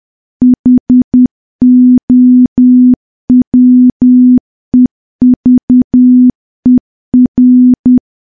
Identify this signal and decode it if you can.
Morse code, heard as a keyed tone, about 10 wpm, 250 Hz. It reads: HOWEVER
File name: morse / morse-training-10WPM